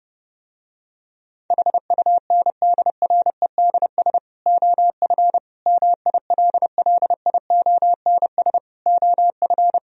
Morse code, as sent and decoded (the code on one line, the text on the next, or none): .... ..- -. -.. .-. -.. ... / --- ..-. / -- .. .-.. .-.. .. --- -. ... / --- ..-.
HUNDREDS OF MILLIONS OF